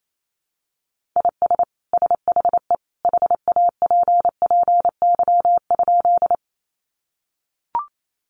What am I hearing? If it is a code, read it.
Morse code, heard as a keyed tone, about 28 wpm, 700 Hz. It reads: IS SHE HAPPY?
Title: Sentences from Top 1000 Words 28wpm